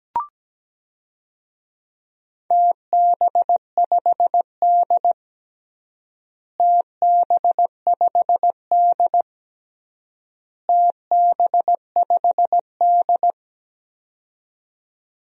Morse code, heard as a keyed tone, about 17 wpm, 700 Hz. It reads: TB5D TB5D TB5D
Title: DXCC-100 Call Signs - 4 Characters - Repeated 3x - 17wpm